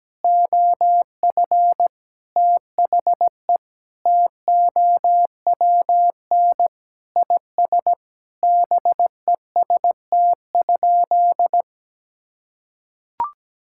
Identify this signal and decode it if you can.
Morse code, heard as a keyed tone, about 17 wpm, 700 Hz. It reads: OF THE TOWN IS BEST?
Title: Sentences from Top 1000 Words - Review Part 5 - 17wpm